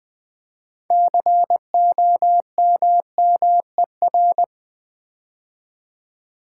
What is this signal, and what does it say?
Morse code, heard as a keyed tone, about 20 wpm, 700 Hz. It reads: COMMER
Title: Build A Word - 2 Letters At A Time - 8-Letter Words - 20wpm